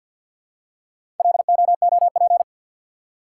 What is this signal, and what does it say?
Morse code, heard as a keyed tone, about 52 wpm, 700 Hz. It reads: LOOP
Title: Top 180 QSO Elements 50wpm